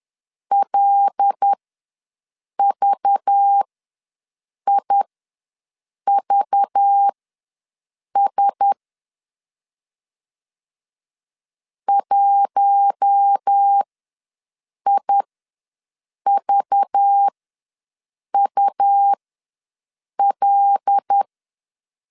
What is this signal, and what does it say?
Morse code, heard as a keyed tone, about 10 wpm, spaced as for 6 wpm, 800 Hz. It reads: LVIVS 1IVUL
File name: LEZIONI / 5 LBVC / 5 LBV ws 10wpm 3645